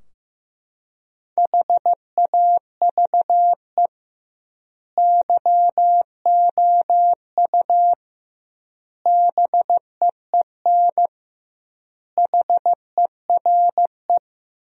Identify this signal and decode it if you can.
Morse code, heard as a keyed tone, about 15 wpm, 700 Hz. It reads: HAVE YOU BEEN HERE
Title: 2x Word-Spacing - Sentences from Top 200 Words - 15wpm